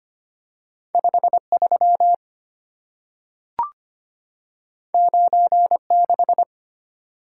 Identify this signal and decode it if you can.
Morse code, heard as a keyed tone, about 25 wpm, 700 Hz. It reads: 53 96
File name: Two Letter-Number Combinations - 25wpm